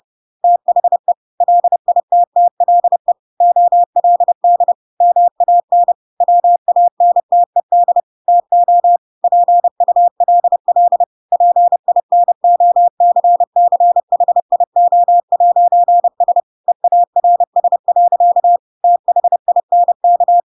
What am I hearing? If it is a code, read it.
Morse code, heard as a keyed tone, about 30 wpm, 700 Hz. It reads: THE LITTLE OLD MAN WANTED TO PULL PINOCCHIO'S EARS. THINK
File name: Pinocchio / Pinocchio ch3 30 30 ik1hge